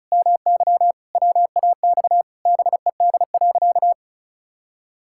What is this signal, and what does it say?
Morse code, heard as a keyed tone, about 36 wpm, 700 Hz. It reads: MY WAX BED.
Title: Pangram Challenge v2 - 35wpm